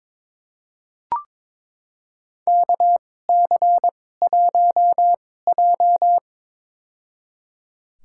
Morse code, heard as a keyed tone, about 21 wpm, spaced as for 16 wpm, 700 Hz. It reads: KC1J
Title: USA Call Signs - 4 Characters - 2x Character Spacing - 22wpm